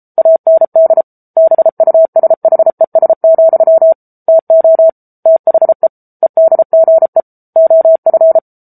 Morse code, heard as a keyed tone, about 34 wpm, 650 Hz. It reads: AND BUSHES, TO THE EDGE OF
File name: Book Two / 07-Chapter 7 - 33 WPM